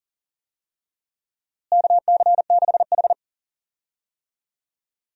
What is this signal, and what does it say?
Morse code, heard as a keyed tone, about 40 wpm, 700 Hz. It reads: KC6H
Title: USA Call Signs - 4 Characters - Repeated 2x - 40wpm